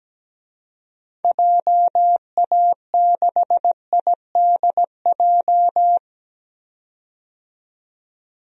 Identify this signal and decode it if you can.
Morse code, heard as a keyed tone, about 17 wpm, 700 Hz. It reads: JA6IDJ
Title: DXCC-100 Call Signs - 6 Characters - 17wpm